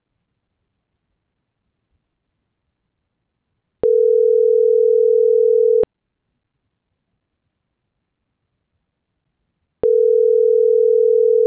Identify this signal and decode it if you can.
Ringback tone (North America), 440 + 480 Hz.
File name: CENTRAL MABE / out saby.reyes25 - .92345